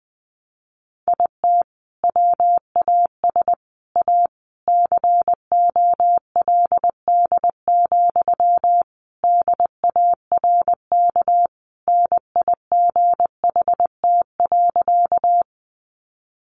Morse code, heard as a keyed tone, about 20 wpm, 700 Hz. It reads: IT WAS A COLD, DARK NIGHT.
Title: Sentences from Top 500 Words 20wpm